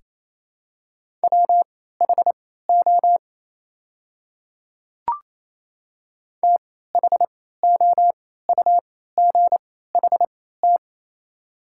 Morse code, heard as a keyed tone, about 28 wpm, spaced as for 16 wpm, 700 Hz. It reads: WHO THOUGHT
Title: 3x Character Spacing - Top 500 Words - 28wpm